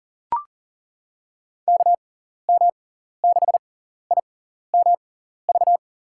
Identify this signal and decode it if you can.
Morse code, heard as a keyed tone, about 40 wpm, spaced as for 14 wpm, 700 Hz. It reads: KM6IMV
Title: USA Call Signs - 6 Characters - 6x Character Spacing - 40wpm